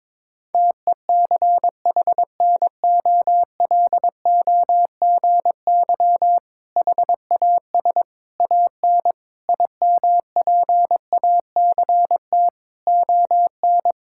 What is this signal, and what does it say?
Morse code, heard as a keyed tone, about 22 wpm, 700 Hz. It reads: TECHNOLOGY HAS AN IMPACT ON
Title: Sentences from Top 1000 Words - Review Part 6 - 22wpm